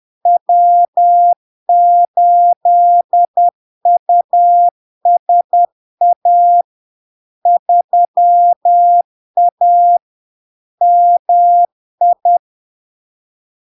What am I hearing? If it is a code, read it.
Morse code, heard as a keyed tone, about 10 wpm, 700 Hz. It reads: W8USA 3A MI
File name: LARK-FD-2022-Log-in-CW-10wpm-700-Hz